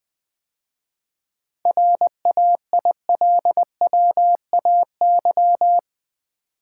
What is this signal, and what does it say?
Morse code, heard as a keyed tone, about 20 wpm, 700 Hz. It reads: RAILWAY
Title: 7 Letter Words - First 3 Letters Given - 20wpm